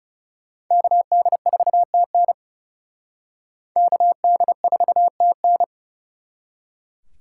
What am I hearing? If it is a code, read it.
Morse code, heard as a keyed tone, about 38 wpm, 700 Hz. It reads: KD4TN KD4TN